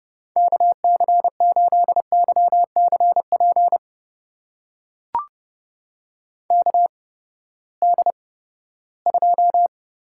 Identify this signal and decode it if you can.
Morse code, heard as a keyed tone, about 30 wpm, spaced as for 17 wpm, 700 Hz. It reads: KC8YCP KD2
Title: USA Call Signs - 6 Characters - 8x Character Spacing - 30wpm